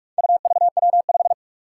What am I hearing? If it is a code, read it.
Morse code, heard as a keyed tone, about 44 wpm, 700 Hz. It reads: UVW5
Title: Four Challenging Letter-Number Combinations 45wpm